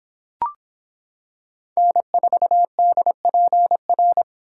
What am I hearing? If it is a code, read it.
Morse code, heard as a keyed tone, about 26 wpm, 700 Hz. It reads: N4DPR